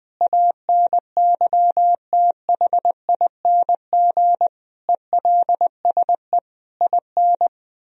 Morse code, heard as a keyed tone, about 21 wpm, 700 Hz. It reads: ANYTHING ELSE IN
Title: Sentences from Top 1000 Words - Review Part 4 - 20wpm